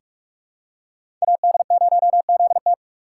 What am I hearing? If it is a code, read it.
Morse code, heard as a keyed tone, about 46 wpm, 700 Hz. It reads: AD0ZT
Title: USA Call Signs - 5 Characters - 3x Character Spacing - 45wpm